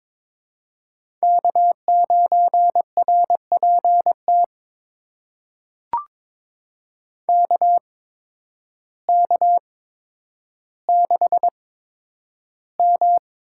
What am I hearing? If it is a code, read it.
Morse code, heard as a keyed tone, about 22 wpm, spaced as for 10 wpm, 700 Hz. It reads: K9RPT KK6M
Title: USA Call Signs - 5 Characters - 8x Character Spacing - 22wpm